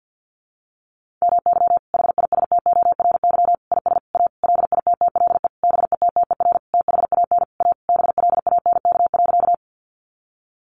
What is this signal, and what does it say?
Morse code, heard as a keyed tone, about 50 wpm, 700 Hz. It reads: MY HISTORY IS A LITTLE BETTER THAN A BLANK.
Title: Sentences from Top 1500 Words 50wpm